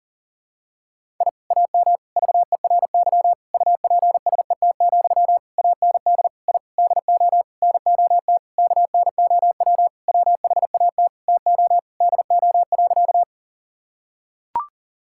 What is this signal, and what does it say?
Morse code, heard as a keyed tone, about 40 wpm, 700 Hz. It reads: I AM VERY UPSET, AND I DO NOT KNOW WHAT TO DO.
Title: Sentences from Top 1000 Words - Review Part 5 - 40wpm